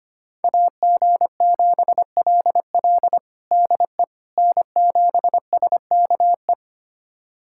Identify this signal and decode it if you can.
Morse code, heard as a keyed tone, about 25 wpm, 700 Hz. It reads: AG7LL DE N7SKE
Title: USA Callsign DE Callsign - 5 Characters - 25wpm